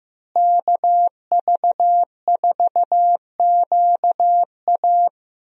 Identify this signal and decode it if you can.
Morse code, heard as a keyed tone, about 15 wpm, 700 Hz. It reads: KV4QA
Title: USA Call Signs - 5 Characters - Repeated 3x - 15wpm